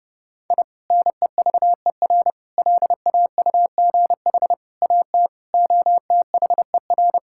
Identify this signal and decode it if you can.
Morse code, heard as a keyed tone, about 32 wpm, 700 Hz. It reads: I NEVER LAUGH AT OTHER
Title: Sentences from Top 1000 Words - Review Part 6 - 30wpm